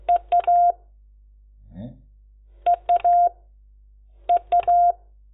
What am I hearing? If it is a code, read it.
Morse code, heard as a keyed tone, about 16 wpm, 700 Hz. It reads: EA EA EA
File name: Lecciones / CLASE Nº10